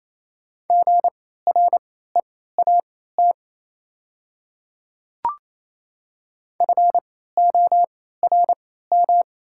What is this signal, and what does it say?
Morse code, heard as a keyed tone, about 28 wpm, spaced as for 16 wpm, 700 Hz. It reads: GREAT FORM
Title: 3x Character Spacing - Top 300 Words - 28wpm